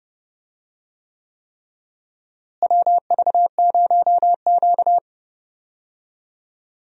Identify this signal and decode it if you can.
Morse code, heard as a keyed tone, about 30 wpm, 700 Hz. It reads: WV0Q
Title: USA Call Signs - 4 Characters - Repeated 3x - 30wpm